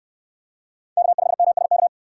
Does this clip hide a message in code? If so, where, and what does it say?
Yes: Morse code from 1.0 to 1.9 s, heard as a keyed tone, about 63 wpm, 700 Hz. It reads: K5MSK